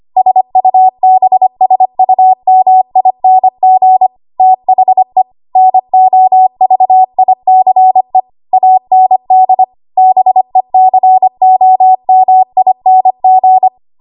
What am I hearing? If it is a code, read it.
Morse code, heard as a keyed tone, about 25 wpm, 750 Hz. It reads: SUBSUMING THE NOVICE AND BECOMING
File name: Archive / 25 WPM / 25WPM